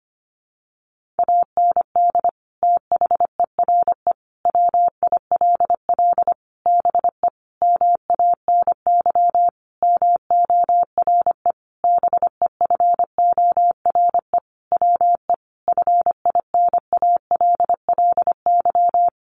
Morse code, heard as a keyed tone, about 25 wpm, 700 Hz. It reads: AND THERE WILL BE MANY MORE BEFORE WE FINALLY
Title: Sentences from Top 1500 Words 25wpm